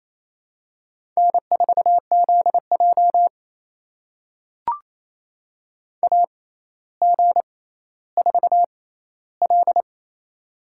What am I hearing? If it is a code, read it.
Morse code, heard as a keyed tone, about 28 wpm, spaced as for 14 wpm, 700 Hz. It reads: N4ZJ AG4L
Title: USA Call Signs - 4 Characters - 6x Character Spacing - 28wpm